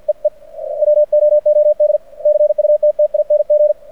Morse code, heard as a keyed tone, about 49 wpm, 600 Hz. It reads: EE WOOD CUTTING
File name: archived / CRSnet